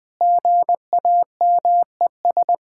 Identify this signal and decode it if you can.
Morse code, heard as a keyed tone, about 19 wpm, 700 Hz. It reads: GAMES
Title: Difficult Sentences with Hints - 20wpm